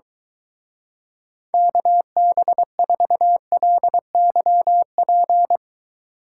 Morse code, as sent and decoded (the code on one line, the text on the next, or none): -.- -... ....- .-.. -.-- .--.
KB4LYP